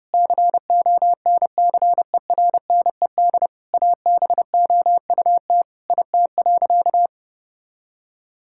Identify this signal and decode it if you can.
Morse code, heard as a keyed tone, about 30 wpm, 700 Hz. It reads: CONCERNED ABOUT IT.
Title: Sentences from Top 1000 Words - Review Part 6 - 30wpm